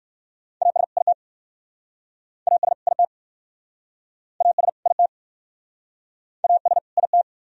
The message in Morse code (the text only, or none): AS IT AS IT AS IT AS IT